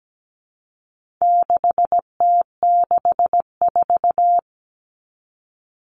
Morse code, heard as a keyed tone, about 17 wpm, 700 Hz. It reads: 6T64
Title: All Cut Numbers - 4-Digit 17wpm